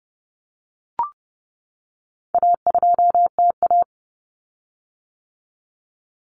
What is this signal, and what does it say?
Morse code, heard as a keyed tone, about 30 wpm, 700 Hz. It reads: A2TA
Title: Common Cut Numbers - 4-Digit 30wpm